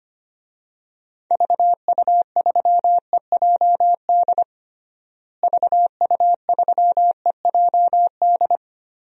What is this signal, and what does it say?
Morse code, heard as a keyed tone, about 25 wpm, 700 Hz. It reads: VU3EJD VU3EJD
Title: DXCC-100 Call Signs - 6 Characters - Repeated 3x - 25wpm